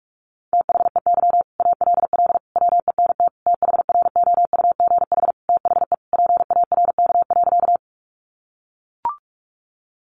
Morse code, heard as a keyed tone, about 44 wpm, 700 Hz. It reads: THEY ALL WENT THROUGH THE PARK.
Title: Sentences from Top 1300 Words 45wpm